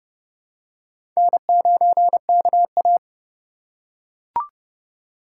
Morse code, heard as a keyed tone, about 29 wpm, 700 Hz. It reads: N9KA
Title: USA Call Signs - 4 Characters - 6x Character Spacing - 30wpm